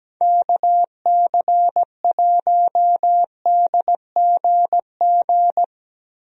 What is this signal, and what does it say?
Morse code, heard as a keyed tone, about 17 wpm, 700 Hz. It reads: KC1DGG
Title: USA Call Signs - 6 Characters - Repeated 2x - 17wpm